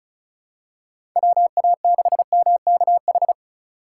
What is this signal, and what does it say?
Morse code, heard as a keyed tone, about 36 wpm, 700 Hz. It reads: WA6MKH
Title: USA Call Signs - 4x Character Spacing - 35wpm